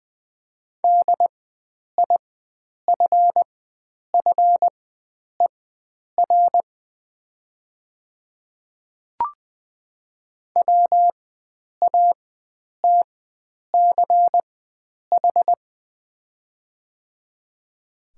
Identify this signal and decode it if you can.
Morse code, heard as a keyed tone, about 20 wpm, spaced as for 9 wpm, 700 Hz. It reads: DIFFER WATCH
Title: 4x Character Spacing - Top 300 Words - 20wpm